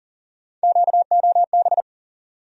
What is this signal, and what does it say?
Morse code, heard as a keyed tone, about 40 wpm, 700 Hz. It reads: QOB
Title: USA Call Sign Postfixes - 3 Letters 40wpm